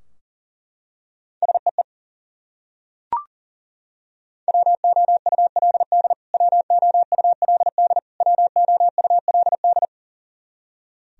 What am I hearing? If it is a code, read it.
Morse code, heard as a keyed tone, about 40 wpm, 700 Hz. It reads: SEE WOULD WOULD WOULD
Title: Top 100 Words Repeated 40wpm